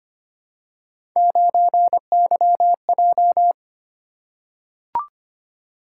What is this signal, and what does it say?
Morse code, heard as a keyed tone, about 26 wpm, 700 Hz. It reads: 9YJ
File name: Three Letter-Number Combinations - 25wpm